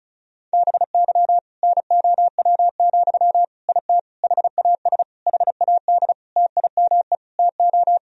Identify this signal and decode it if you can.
Morse code, heard as a keyed tone, about 34 wpm, 700 Hz. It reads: BY NOW, IT HAS HAD TIME TO
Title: Sentences from Top 1000 Words - Review Part 1 - 35wpm